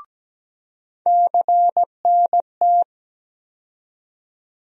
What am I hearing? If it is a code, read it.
Morse code, heard as a keyed tone, about 17 wpm, 700 Hz. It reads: CNT
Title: USA Call Sign Postfixes - 3 Letters 17wpm